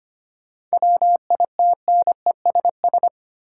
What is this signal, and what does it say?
Morse code, heard as a keyed tone, about 25 wpm, 700 Hz. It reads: WITNESS
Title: Seven Letter Words - 25wpm